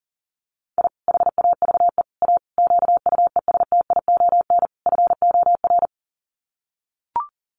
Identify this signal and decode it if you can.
Morse code, heard as a keyed tone, about 40 wpm, 700 Hz. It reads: I HAVE A QUESTION FOR
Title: Sets of 5 Words Encore2 40wpm